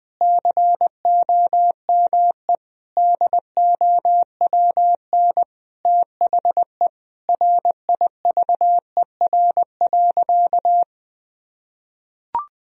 Morse code, heard as a keyed tone, about 20 wpm, 700 Hz. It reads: COME DOWN THE RIVER.
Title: Sentences from Top 1000 Words - Review Part 2 - 20wpm